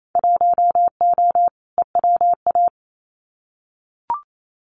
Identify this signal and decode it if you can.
Morse code, heard as a keyed tone, about 28 wpm, 700 Hz. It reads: 1O EWA
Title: Winter Field Day Exchanges Encore 28wpm